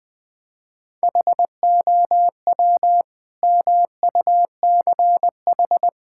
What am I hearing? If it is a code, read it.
Morse code, heard as a keyed tone, about 21 wpm, 700 Hz. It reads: HOW MUCH
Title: Sentences from Top 1000 Words - Review Part 5 - 20wpm